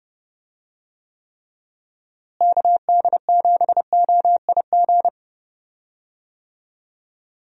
Morse code, heard as a keyed tone, about 31 wpm, 700 Hz. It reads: KD7OIG
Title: USA Call Signs - 6 Characters - Repeated 3x - 30wpm